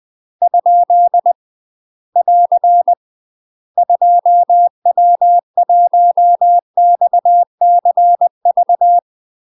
Morse code, heard as a keyed tone, about 20 wpm, 700 Hz. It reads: ? <AR> 2W1XCV
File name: UK MASTER QSOs TRAINING/20WPM